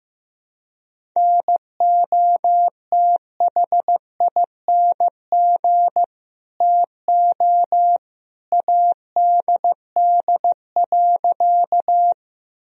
Morse code, heard as a keyed tone, about 15 wpm, 700 Hz. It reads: NOTHING TO ADD.
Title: Sentences from Top 1000 Words - Review Part 1 - 15wpm